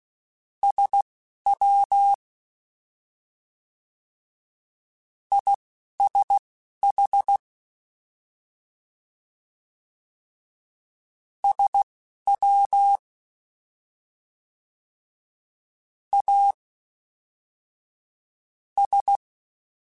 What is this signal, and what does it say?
Morse code, heard as a keyed tone, about 16 wpm, spaced as for 5 wpm, 800 Hz. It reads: SWISH SWAS